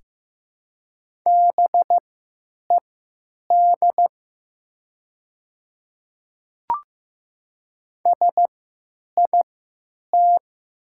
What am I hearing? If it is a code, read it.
Morse code, heard as a keyed tone, about 15 wpm, spaced as for 9 wpm, 700 Hz. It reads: BED SIT
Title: 3x Character Spacing - Top 500 Words - 15wpm